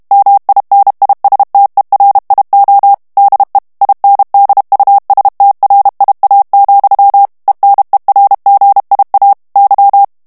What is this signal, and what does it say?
Morse code, heard as a keyed tone, about 32 wpm, 800 Hz. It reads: MINISTERIO DE INDUSTRIA, ENERGIA Y
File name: ESPAGNOL TEXTE EN CLAIR 2 VITESSE 30 MOTS MINUTES